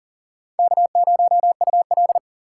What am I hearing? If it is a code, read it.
Morse code, heard as a keyed tone, about 53 wpm, 700 Hz. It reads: K0UL